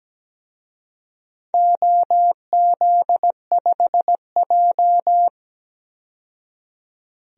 Morse code, heard as a keyed tone, about 17 wpm, 700 Hz. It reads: OZ5J
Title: DXCC-100 Call Signs - 4 Characters - 17wpm